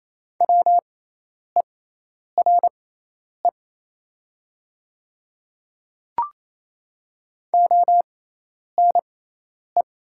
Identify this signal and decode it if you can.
Morse code, heard as a keyed tone, about 28 wpm, spaced as for 10 wpm, 700 Hz. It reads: WERE ONE
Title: 6x Character Spacing - Top 300 Words - 28wpm